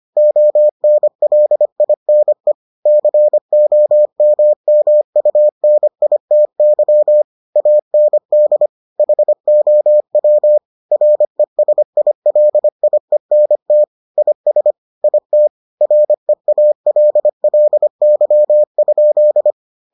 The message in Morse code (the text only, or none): ONLINE COMMUNITY AND HOW RESILIENT IS IT REALLY?